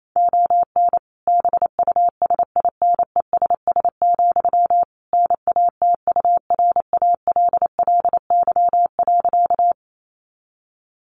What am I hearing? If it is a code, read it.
Morse code, heard as a keyed tone, about 28 wpm, 700 Hz. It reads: ON BUSINESS, NATURALLY.
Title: Sentences from Top 1300 Words 28wpm